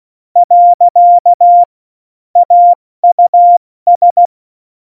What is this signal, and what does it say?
Morse code, heard as a keyed tone, about 16 wpm, 700 Hz. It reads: . AUS
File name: Morsekurs / Witze / Witz 11. Februar 2020, 70 F80